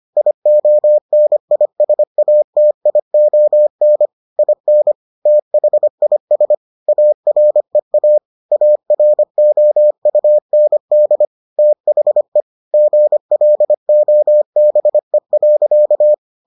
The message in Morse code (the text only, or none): IONISATION IN THIS AREA AROUND THE GLOBE.